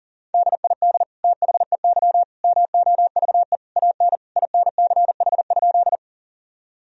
Morse code, heard as a keyed tone, about 40 wpm, 700 Hz. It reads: DID THEY MOVE AN INCH?
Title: Sentences from Top 1000 Words - Review Part 3 - 40wpm